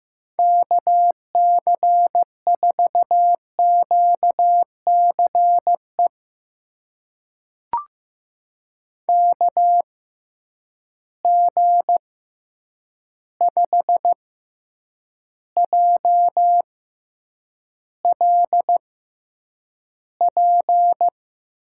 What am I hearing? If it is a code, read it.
Morse code, heard as a keyed tone, about 15 wpm, spaced as for 8 wpm, 700 Hz. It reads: KC4QCE KG5JLP
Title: USA Call Signs - 6 Characters - 6x Character Spacing - 15wpm